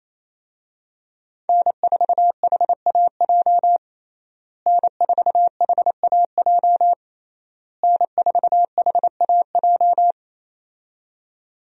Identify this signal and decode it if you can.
Morse code, heard as a keyed tone, about 28 wpm, 700 Hz. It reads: N4HAJ N4HAJ N4HAJ
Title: USA Call Signs - 5 Characters - Repeated 3x - 28wpm